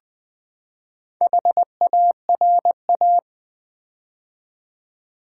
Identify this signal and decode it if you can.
Morse code, heard as a keyed tone, about 20 wpm, 700 Hz. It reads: HARA
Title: Build A Word - 4 Letters At A Time - 10-Letter Words - 20wpm